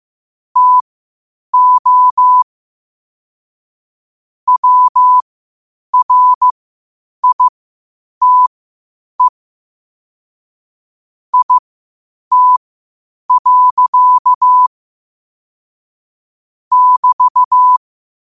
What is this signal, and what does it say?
Morse code, heard as a keyed tone, about 15 wpm, spaced as for 8 wpm, 1000 Hz. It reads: TO WRITE IT. <BT>
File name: qotdfiles / 8wpm